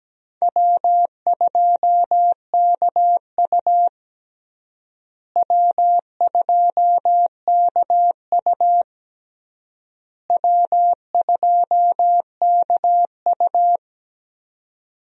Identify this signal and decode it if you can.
Morse code, heard as a keyed tone, about 17 wpm, 700 Hz. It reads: W2KU W2KU W2KU
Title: USA Call Signs - 4 Characters - Repeated 3x - 17wpm